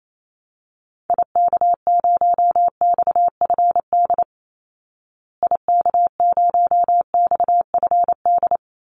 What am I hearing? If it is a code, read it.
Morse code, heard as a keyed tone, about 28 wpm, 700 Hz. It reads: IK0XFD IK0XFD
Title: Call Signs Repeated 3x - 28wpm